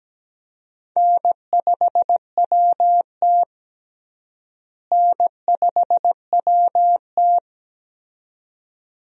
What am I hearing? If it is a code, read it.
Morse code, heard as a keyed tone, about 17 wpm, 700 Hz. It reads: N5WT N5WT
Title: USA Call Signs - 4 Characters - Repeated 2x - 17wpm